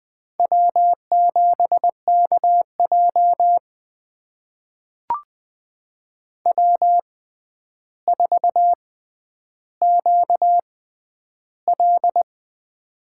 Morse code, heard as a keyed tone, about 20 wpm, spaced as for 10 wpm, 700 Hz. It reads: W7KJ W4QL
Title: USA Call Signs - 6x Character Spacing - 20wpm